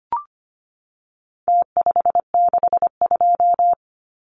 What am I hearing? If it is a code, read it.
Morse code, heard as a keyed tone, about 25 wpm, 700 Hz. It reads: T562
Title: Numbers - Contest Style - 4-Digit 25wpm